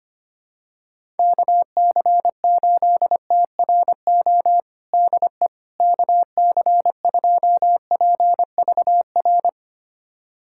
Morse code, heard as a keyed tone, about 25 wpm, 700 Hz. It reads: KC8TRO DE KC2PVR
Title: USA Callsign DE Callsign - 6 Characters - 25wpm